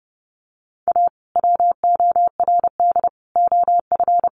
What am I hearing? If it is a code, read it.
Morse code, heard as a keyed tone, about 30 wpm, 700 Hz. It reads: A WORD OF
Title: Sentences from Top 1100 Words 30wpm